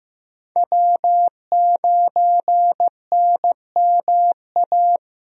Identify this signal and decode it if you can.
Morse code, heard as a keyed tone, about 15 wpm, 700 Hz. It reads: W9NMA
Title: USA Call Signs - 5 Characters - Repeated 2x - 15wpm